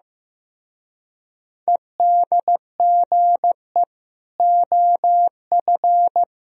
Morse code, heard as a keyed tone, about 17 wpm, 700 Hz. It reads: EDGE OF
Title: Speed-Racing - Sets of 3 Words 15wpm